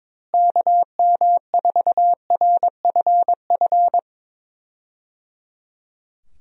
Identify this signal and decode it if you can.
Morse code, heard as a keyed tone, about 21 wpm, 700 Hz. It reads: KM4RFF
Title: USA Call Signs - 6 Characters - Repeated 3x - 22wpm